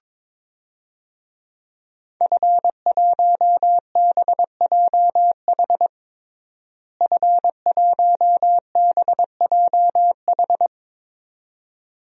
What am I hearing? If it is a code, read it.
Morse code, heard as a keyed tone, about 22 wpm, 700 Hz. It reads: F1BJH F1BJH
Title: DXCC-100 Call Signs - 5 Characters - Repeated 2x - 22wpm